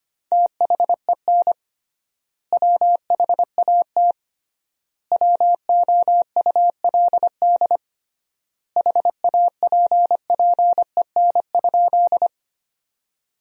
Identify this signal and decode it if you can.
Morse code, heard as a keyed tone, about 25 wpm, 700 Hz. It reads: THEN WHAT WOULD HAPPEN?
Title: 3x Word-Spacing - Sentences from Top 400 Words - 25wpm